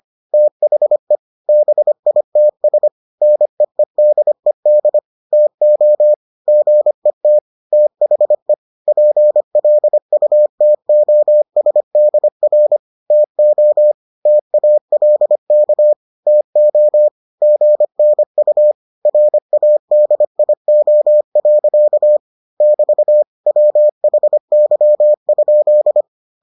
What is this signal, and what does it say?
Morse code, heard as a keyed tone, about 25 wpm, 600 Hz. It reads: THE BITS NEEDED TO GET THE PLUTOSDR TO TALK TO GNU RADIO. <BT> WHY?